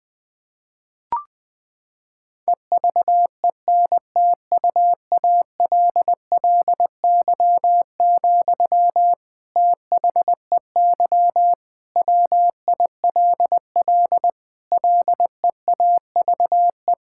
Morse code, heard as a keyed tone, about 20 wpm, 700 Hz. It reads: EVENTUALLY, THEY WILL LEAVE
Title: Sentences from Top 1000 Words - Review Part 6 - 20wpm